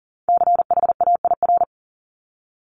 Morse code, heard as a keyed tone, about 42 wpm, 690 Hz. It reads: CHAIR
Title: Multi-letter Phonemes 40wpm